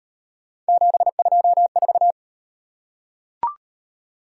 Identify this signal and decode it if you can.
Morse code, heard as a keyed tone, about 37 wpm, 700 Hz. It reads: 724